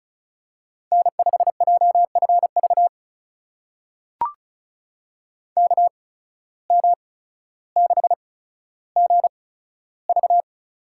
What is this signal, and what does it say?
Morse code, heard as a keyed tone, about 34 wpm, spaced as for 15 wpm, 700 Hz. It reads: N5JFV KM6GV
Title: USA Call Signs - 5 Characters - 8x Character Spacing - 35wpm